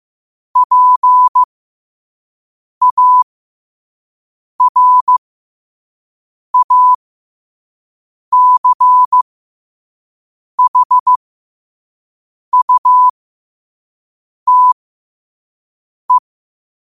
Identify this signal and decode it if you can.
Morse code, heard as a keyed tone, about 15 wpm, spaced as for 5 wpm, 1000 Hz. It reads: PARACHUTE
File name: qotdfiles / 5wpm